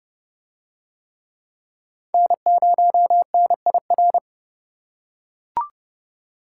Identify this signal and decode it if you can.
Morse code, heard as a keyed tone, about 29 wpm, 700 Hz. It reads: N0NIR